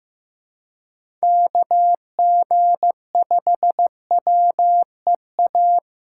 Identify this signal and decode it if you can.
Morse code, heard as a keyed tone, about 15 wpm, 700 Hz. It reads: KG5WEA
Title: USA Call Signs - 6 Characters - 4x Character Spacing - 15wpm